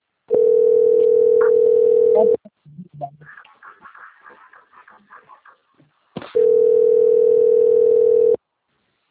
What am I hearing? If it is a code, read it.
Ringback tone (North America), 440 + 480 Hz.
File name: rg-1600-unknown- - - .4965